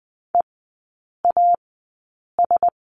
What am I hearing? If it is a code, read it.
Morse code, heard as a keyed tone, about 20 wpm, spaced as for 8 wpm, 700 Hz. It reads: EAS
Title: 035 Lesson 7e - Letter s - Three Characters 20wpm